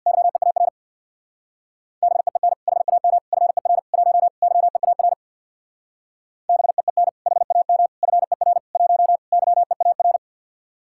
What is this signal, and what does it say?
Morse code, heard as a keyed tone, about 66 wpm, 700 Hz. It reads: YEAR BEEN HAM FER 1 YEAR BEEN HAM FER 1 YEAR